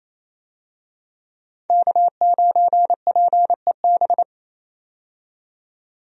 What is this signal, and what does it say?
Morse code, heard as a keyed tone, about 28 wpm, 700 Hz. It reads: K9PEB
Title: USA Call Signs - 5 Characters - 28wpm